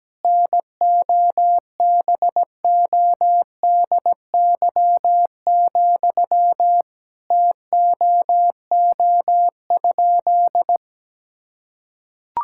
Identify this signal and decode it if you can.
Morse code, heard as a keyed tone, about 17 wpm, 700 Hz. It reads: NOBODY, TOO?
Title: Sentences from Top 1600 Words 17wpm